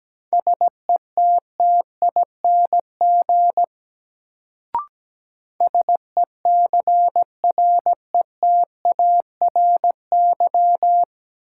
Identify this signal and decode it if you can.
Morse code, heard as a keyed tone, about 17 wpm, 700 Hz. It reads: SETTING SECRETARY
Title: Sentences from Top 1300 Words - New Word Drill - 17wpm